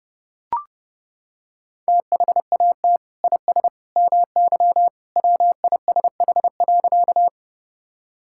Morse code, heard as a keyed tone, about 32 wpm, 700 Hz. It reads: THAT IS MY WISH.
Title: Sentences from Top 1600 Words 30wpm